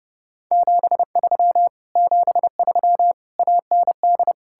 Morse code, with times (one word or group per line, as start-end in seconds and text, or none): 0.5-1.7 73
2.0-3.1 73
3.4-4.3 AND